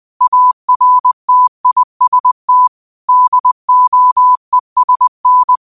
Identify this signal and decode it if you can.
Morse code, heard as a keyed tone, about 20 wpm, 1000 Hz. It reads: ARTIST DOESN
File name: qotdfiles / 20wpm